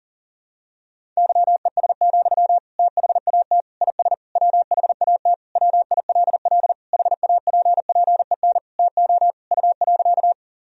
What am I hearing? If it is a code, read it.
Morse code, heard as a keyed tone, about 40 wpm, 700 Hz. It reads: YES, THAT IS WHAT WILL HAPPEN TO U.
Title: Sentences from Top 500 Words and Abbreviations 40wpm